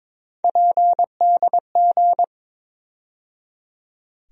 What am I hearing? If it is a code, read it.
Morse code, heard as a keyed tone, about 22 wpm, 700 Hz. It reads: PDG
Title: USA Call Sign Postfixes - 3 Letters 22wpm